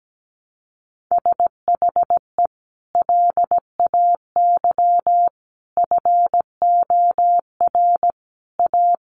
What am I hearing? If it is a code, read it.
Morse code, heard as a keyed tone, about 17 wpm, 700 Hz. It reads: SHE LAY FOR A
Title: Sentences from Top 900 Words 17wpm